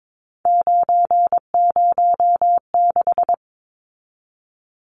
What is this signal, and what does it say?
Morse code, heard as a keyed tone, about 22 wpm, 700 Hz. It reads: 906
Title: Numbers - 3-Digit 22wpm